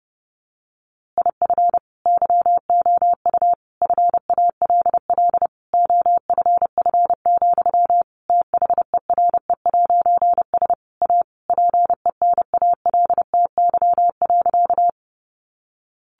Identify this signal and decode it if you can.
Morse code, heard as a keyed tone, about 30 wpm, 700 Hz. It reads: IF YOU FALL OFF, THERE'S A PENALTY.
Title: A Sound Of Thunder 30wpm